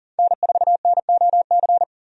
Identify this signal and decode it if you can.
Morse code, heard as a keyed tone, about 40 wpm, 700 Hz. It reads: N4NOC
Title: USA Call Signs - 5 Characters - 40wpm